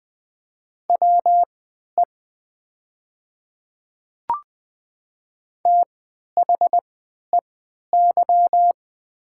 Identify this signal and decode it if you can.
Morse code, heard as a keyed tone, about 20 wpm, spaced as for 11 wpm, 700 Hz. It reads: WE THEY
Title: 3x Character Spacing - Top 100 Words - 20wpm